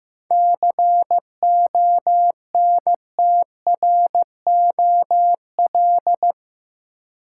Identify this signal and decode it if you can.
Morse code, heard as a keyed tone, about 15 wpm, 700 Hz. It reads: CONTROL
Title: Sentences from Top 1000 Words - Review Part 2 - New Word Drill - 15wpm